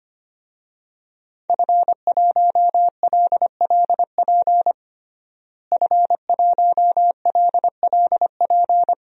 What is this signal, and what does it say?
Morse code, heard as a keyed tone, about 25 wpm, 700 Hz. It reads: F1LLP F1LLP
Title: DXCC-100 Call Signs - 5 Characters - Repeated 2x - 25wpm